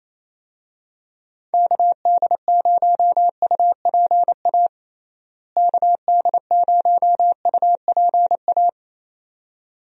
Morse code, heard as a keyed tone, about 27 wpm, 700 Hz. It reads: KD0UPA KD0UPA
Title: USA Call Signs - 6 Characters - Repeated 2x - 28wpm